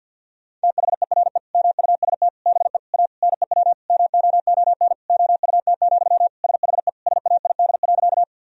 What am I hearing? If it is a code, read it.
Morse code, heard as a keyed tone, about 50 wpm, 700 Hz. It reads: THERE MUST BE A NEW MOON OUT, SHE SAID.
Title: Sentences from Top 1000 Words - Review Part 3 - 50wpm